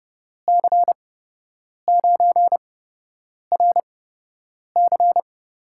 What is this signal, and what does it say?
Morse code, heard as a keyed tone, about 30 wpm, spaced as for 8 wpm, 700 Hz. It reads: C9RC
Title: USA Call Signs - 6 Characters - 8x Character Spacing - 30wpm